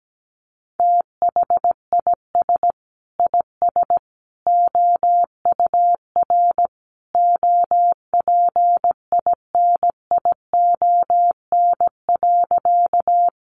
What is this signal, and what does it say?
Morse code, heard as a keyed tone, about 17 wpm, 700 Hz. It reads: THIS IS OUR OPINION.
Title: Sentences from Top 1200 Words 17wpm